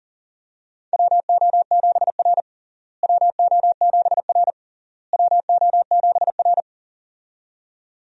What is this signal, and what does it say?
Morse code, heard as a keyed tone, about 40 wpm, 700 Hz. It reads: WO7R WO7R WO7R
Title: USA Call Signs - 4 Characters - Repeated 3x - 40wpm